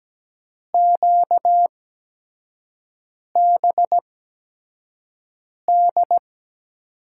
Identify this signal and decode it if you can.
Morse code, heard as a keyed tone, about 17 wpm, spaced as for 5 wpm, 700 Hz. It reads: QBD
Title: USA Call Signs - 8x Character Spacing - 17wpm